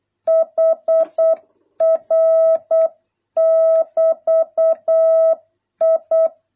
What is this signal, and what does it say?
Morse code, heard as a keyed tone, about 8 wpm, 650 Hz. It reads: HR <BT> I